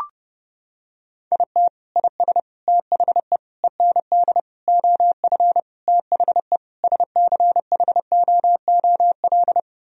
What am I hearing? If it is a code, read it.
Morse code, heard as a keyed tone, about 30 wpm, 700 Hz. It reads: IT IS THE END OF THE SCHOOL
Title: Sentences from Top 1000 Words - Review Part 1 - 30wpm